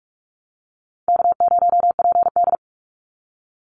Morse code, heard as a keyed tone, about 44 wpm, 700 Hz. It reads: K0PD
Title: Easy Call Signs Repeated 3x - 45wpm